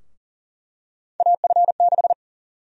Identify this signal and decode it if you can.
Morse code, heard as a keyed tone, about 40 wpm, 700 Hz. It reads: AF6
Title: USA Call Sign Prefixes - 40wpm